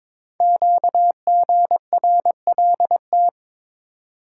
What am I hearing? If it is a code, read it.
Morse code, heard as a keyed tone, about 23 wpm, 700 Hz. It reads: QGRLT
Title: Code Groups - 5 Letters - 22wpm